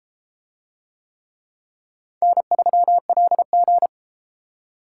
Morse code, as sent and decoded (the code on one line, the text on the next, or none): -. ...-- .-.. --.
N3LG